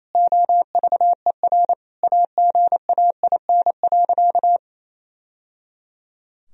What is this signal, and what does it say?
Morse code, heard as a keyed tone, about 28 wpm, 700 Hz. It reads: OVER AGAIN.
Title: Sentences from Top 1000 Words - Review Part 1 - 28wpm